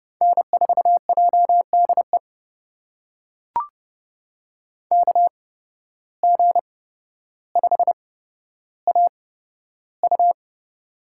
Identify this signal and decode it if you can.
Morse code, heard as a keyed tone, about 31 wpm, spaced as for 13 wpm, 700 Hz. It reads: N4JDE KG5AU
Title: USA Call Signs - 5 Characters - 8x Character Spacing - 30wpm